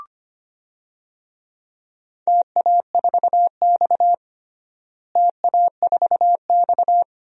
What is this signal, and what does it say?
Morse code, heard as a keyed tone, about 25 wpm, 700 Hz. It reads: TA4X TA4X
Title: DXCC-100 Call Signs - 4 Characters - Repeated 2x - 25wpm